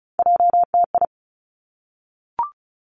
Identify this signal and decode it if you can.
Morse code, heard as a keyed tone, about 36 wpm, 700 Hz. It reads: JTI
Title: Three Challenging Letter-Number Combinations 35wpm